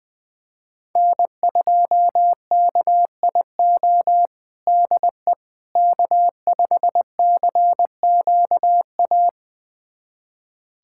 Morse code, heard as a keyed tone, about 21 wpm, 700 Hz. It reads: N2KIO DE K5CQA
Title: USA Callsign DE Callsign - 5 Characters - 20wpm